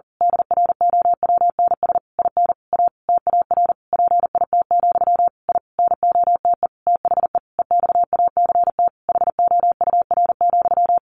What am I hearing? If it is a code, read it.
Morse code, heard as a keyed tone, about 40 wpm, 700 Hz. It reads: DROWNS IN A TAR PIT, I NOTE THE EXACT HOUR,
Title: A Sound Of Thunder 40wpm